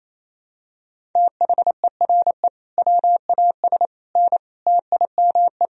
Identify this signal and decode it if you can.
Morse code, heard as a keyed tone, about 28 wpm, 700 Hz. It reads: THERE WAS N TIME
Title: Sentences from Top 400 Words and Abbreviations 28wpm